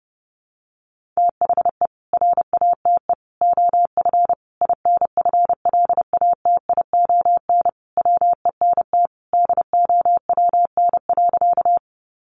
Morse code, heard as a keyed tone, about 30 wpm, 700 Hz. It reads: THE RATE OF INFLATION WENT DOWN.
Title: Sentences from Top 1100 Words 30wpm